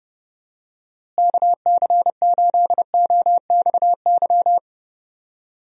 Morse code, heard as a keyed tone, about 30 wpm, 700 Hz. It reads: KC8OXY
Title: USA Call Signs - 6 Characters - Repeated 3x - 30wpm